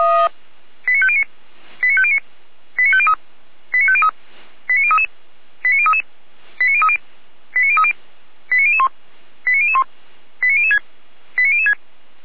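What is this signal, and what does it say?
Five-tone selective call (ZVEI), heard as quick runs of five tones, tones of about 70 ms. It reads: 89509 89509 89503 89503 89033 89033 89030 89030 89001 89001 89006 89006
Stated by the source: ZVEI five-tone selective call received via SDR, from the Feuerwehr Orlamünde fire brigade audio server.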